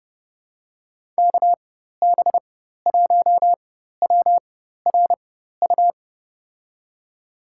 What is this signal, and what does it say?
Morse code, heard as a keyed tone, about 32 wpm, spaced as for 14 wpm, 700 Hz. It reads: KB1WRU
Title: USA Call Signs - 6 Characters - 4x Character Spacing - 30wpm